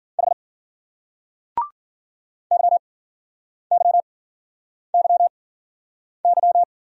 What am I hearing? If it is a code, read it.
Morse code, heard as a keyed tone, about 50 wpm, spaced as for 9 wpm, 700 Hz. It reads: H YYYY
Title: Speed-Racing Single Letters 40wpm